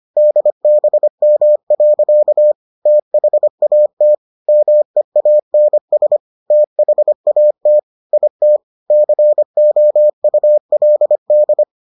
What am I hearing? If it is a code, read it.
Morse code, heard as a keyed tone, about 25 wpm, 600 Hz. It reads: DBM. THAT MEANS THAT IT COULD